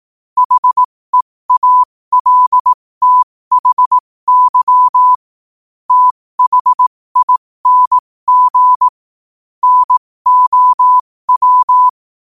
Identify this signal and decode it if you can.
Morse code, heard as a keyed tone, about 18 wpm, spaced as for 15 wpm, 1000 Hz. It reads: HEALTHY THING NOW
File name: qotdfiles / 15wpm